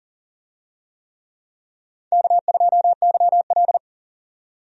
Morse code, heard as a keyed tone, about 40 wpm, 700 Hz. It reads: K2YL
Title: USA Call Signs - 4 Characters - 40wpm